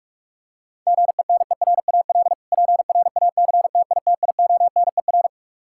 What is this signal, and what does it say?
Morse code, heard as a keyed tone, about 46 wpm, 700 Hz. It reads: GENERAL PRACTITIONER
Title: Practice QSO - Exchange 3 - More Info - 45wpm